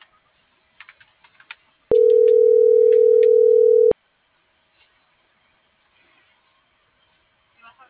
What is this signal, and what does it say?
Ringback tone (North America), 440 + 480 Hz.